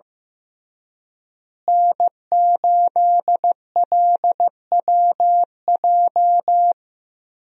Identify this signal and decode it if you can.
Morse code, heard as a keyed tone, about 15 wpm, 700 Hz. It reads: N8LWJ
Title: USA Call Signs - 5 Characters - Repeated 3x - 15wpm